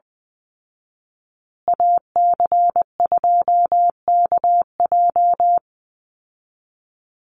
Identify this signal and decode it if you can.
Morse code, heard as a keyed tone, about 20 wpm, 700 Hz. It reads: AC2KJ
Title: Call Signs Repeated 2x - 20wpm